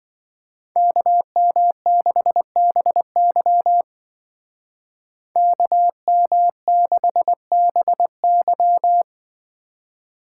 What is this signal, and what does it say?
Morse code, heard as a keyed tone, about 24 wpm, 700 Hz. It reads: KM6BY KM6BY
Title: Speed Racing - USA 2x2 Call Signs 20wpm